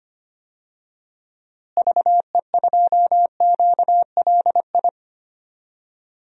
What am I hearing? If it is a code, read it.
Morse code, heard as a keyed tone, about 25 wpm, 700 Hz. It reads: VE2QLI
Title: DXCC-100 Call Signs - 6 Characters - Repeated 3x - 25wpm